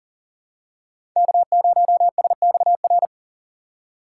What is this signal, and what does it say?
Morse code, heard as a keyed tone, about 40 wpm, 700 Hz. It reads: K0SXR
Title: USA Call Signs - 4x Character Spacing - 40wpm